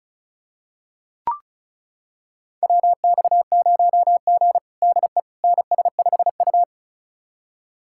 Morse code, heard as a keyed tone, about 36 wpm, 700 Hz. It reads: WX0G DE NS5U
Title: USA Callsign DE Callsign - 4 Characters - 35wpm